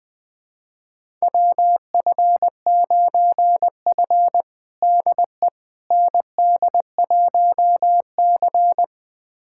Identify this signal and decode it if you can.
Morse code, heard as a keyed tone, about 20 wpm, 700 Hz. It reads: WF9F DE ND1C
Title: USA Callsign DE Callsign - 4 Characters - 20wpm